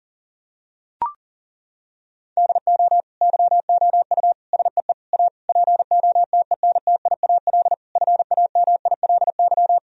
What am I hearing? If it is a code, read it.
Morse code, heard as a keyed tone, about 40 wpm, 700 Hz. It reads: DO YOU SEE A POTENTIAL FAMILY
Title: Sentences from Top 1000 Words - Review Part 4 - 40wpm